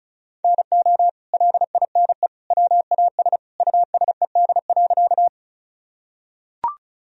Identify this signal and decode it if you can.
Morse code, heard as a keyed tone, about 34 wpm, 700 Hz. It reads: NO LINE WAS USED.
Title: Sentences from Top 1000 Words - Review Part 1 - 35wpm